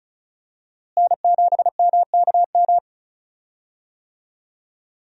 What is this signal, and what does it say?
Morse code, heard as a keyed tone, about 34 wpm, 700 Hz. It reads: N7MKM
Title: USA Call Signs - 5 Characters - Repeated 2x - 35wpm